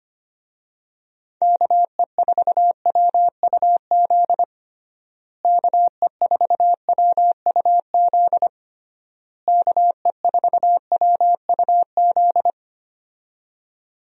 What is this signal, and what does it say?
Morse code, heard as a keyed tone, about 25 wpm, 700 Hz. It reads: KE4WUZ KE4WUZ KE4WUZ
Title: USA Call Signs - 6 Characters - Repeated 3x - 25wpm